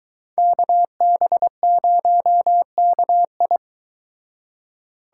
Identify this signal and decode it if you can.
Morse code, heard as a keyed tone, about 23 wpm, 700 Hz. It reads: KB0KI